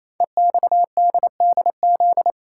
Morse code, heard as a keyed tone, about 27 wpm, 700 Hz. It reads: EXDDZ
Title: Code Groups - 5 Letters - 28wpm